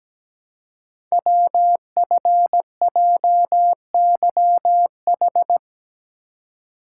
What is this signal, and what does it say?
Morse code, heard as a keyed tone, about 17 wpm, 700 Hz. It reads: WFJYH
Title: Code Groups - 5 Letters - 17wpm